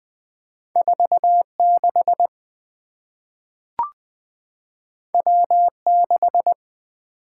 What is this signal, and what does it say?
Morse code, heard as a keyed tone, about 20 wpm, 700 Hz. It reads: 4 6 W 6
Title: Two Letter-Number Combinations - 20wpm